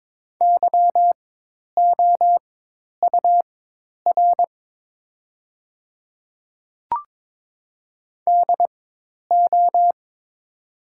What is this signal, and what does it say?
Morse code, heard as a keyed tone, about 22 wpm, spaced as for 10 wpm, 700 Hz. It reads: YOUR DO
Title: 4x Character Spacing - Top 100 Words - 22wpm